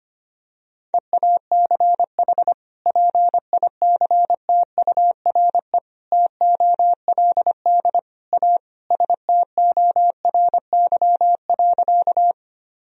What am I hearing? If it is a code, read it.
Morse code, heard as a keyed tone, about 25 wpm, 700 Hz. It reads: EACH PICTURE TOLD A STORY.
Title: Sentences from Top 1000 Words - Review Part 1 - 25wpm